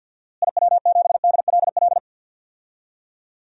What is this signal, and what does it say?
Morse code, heard as a keyed tone, about 47 wpm, 700 Hz. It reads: IW7DLL
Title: DXCC-100 Call Signs - 6 Characters - Repeated 3x - 50wpm